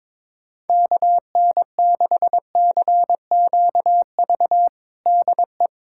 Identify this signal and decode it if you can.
Morse code, heard as a keyed tone, about 22 wpm, 700 Hz. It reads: KN6CQV DE
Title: USA Callsign DE Callsign - 6 Characters - 22wpm